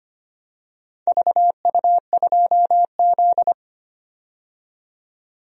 Morse code, heard as a keyed tone, about 25 wpm, 700 Hz. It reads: VU2Z
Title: DXCC-100 Call Signs - 4 Characters - 25wpm